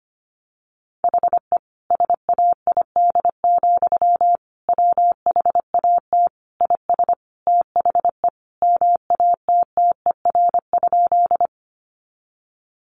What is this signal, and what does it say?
Morse code, heard as a keyed tone, about 25 wpm, 700 Hz. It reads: HE SAID, WHAT IS THE MATTER?
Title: Sentences from Top 700 Words 25wpm